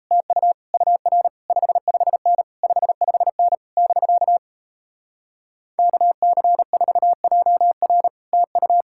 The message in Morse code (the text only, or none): TU UR 55N 55N <BK> KC4JR TU